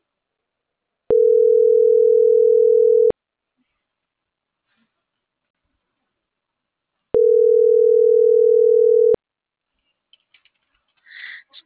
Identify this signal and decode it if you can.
Ringback tone (North America), 440 + 480 Hz.